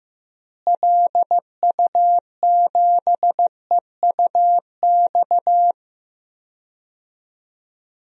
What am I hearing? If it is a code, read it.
Morse code, heard as a keyed tone, about 15 wpm, 700 Hz. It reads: LU7EUX
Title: DXCC-100 Call Signs - 6 Characters - Repeated 3x - 15wpm